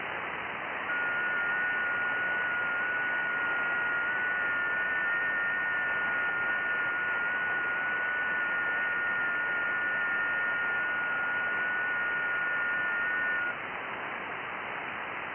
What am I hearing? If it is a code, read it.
FT8.
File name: IC-7100